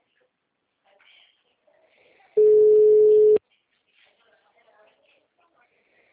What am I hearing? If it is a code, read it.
Ringback tone (Europe, 425 Hz), with speech.